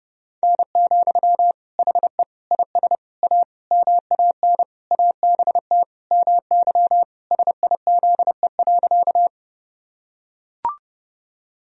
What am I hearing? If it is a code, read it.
Morse code, heard as a keyed tone, about 32 wpm, 700 Hz. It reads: N, HE IS A MAN ABT MY SIZE.
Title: Sentences from Top 400 Words and Abbreviations 30wpm